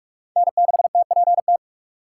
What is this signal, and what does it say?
Morse code, heard as a keyed tone, about 46 wpm, 700 Hz. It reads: N6TPT
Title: USA Call Signs - 5 Characters - 2x Character Spacing - 45wpm